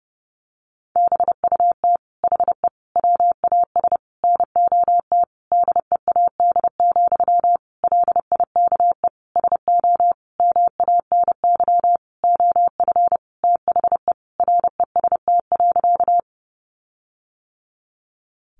Morse code, heard as a keyed tone, about 30 wpm, 700 Hz. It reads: BUT HE WAS NOT DEAD, LIKE SO MANY OF THE REST.
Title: Sentences from Top 1500 Words 30wpm